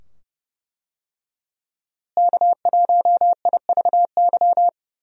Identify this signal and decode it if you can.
Morse code, heard as a keyed tone, about 32 wpm, 700 Hz. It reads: K1IVY
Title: USA Call Signs - 5 Characters - Repeated 2x - 30wpm